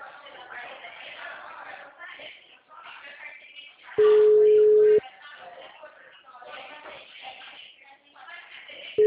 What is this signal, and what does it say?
Ringback tone (Europe, 425 Hz), with speech.
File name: out- -2010- - - .61777